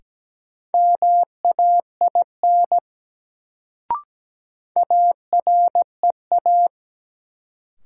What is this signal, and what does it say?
Morse code, heard as a keyed tone, about 17 wpm, 700 Hz. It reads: MAIN AREA
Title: Sentences from Top 1000 Words - Review Part 2 - New Word Drill - 17wpm